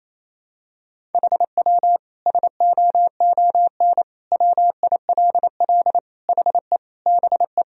Morse code, heard as a keyed tone, about 28 wpm, 700 Hz. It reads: HW SOON WILL HE BE
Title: Sentences from Top 400 Words and Abbreviations 28wpm